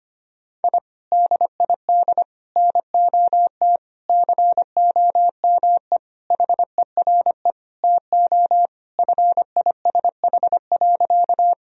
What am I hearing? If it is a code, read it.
Morse code, heard as a keyed tone, about 25 wpm, 700 Hz. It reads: I DID NOT COME HERE TO FISH.
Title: Sentences from Top 1000 Words - Review Part 1 - 25wpm